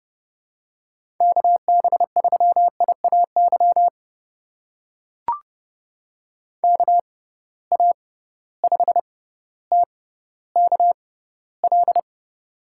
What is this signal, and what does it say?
Morse code, heard as a keyed tone, about 30 wpm, spaced as for 15 wpm, 700 Hz. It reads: KB3IAY KA5TKL
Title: USA Call Signs - 6 Characters - 6x Character Spacing - 30wpm